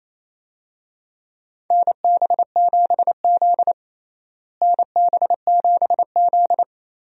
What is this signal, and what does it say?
Morse code, heard as a keyed tone, about 28 wpm, 700 Hz. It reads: NB7Z NB7Z
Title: USA Call Signs - 4 Characters - Repeated 2x - 28wpm